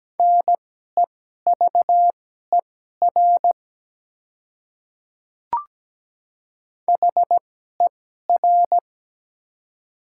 Morse code, heard as a keyed tone, about 17 wpm, spaced as for 12 wpm, 700 Hz. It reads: NEVER HER
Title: 2x Character Spacing - Top 300 Words - 17wpm